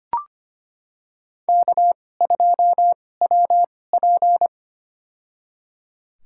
Morse code, heard as a keyed tone, about 25 wpm, spaced as for 18 wpm, 700 Hz. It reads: K2WP
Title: USA Call Signs - 4 Characters - 2x Character Spacing - 25wpm